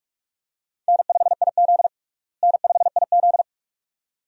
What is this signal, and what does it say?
Morse code, heard as a keyed tone, about 43 wpm, 700 Hz. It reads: N5IZ N5IZ
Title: USA Call Signs - 4 Characters - Repeated 2x - 45wpm